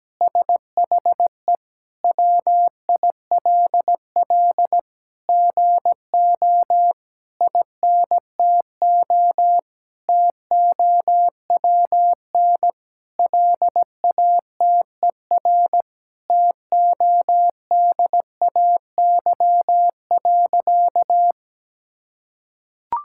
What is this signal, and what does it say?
Morse code, heard as a keyed tone, about 17 wpm, 700 Hz. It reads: SHE WILL GO INTO TOWN LATER TODAY.
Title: Sentences from Top 1000 Words - Review Part 6 - 17wpm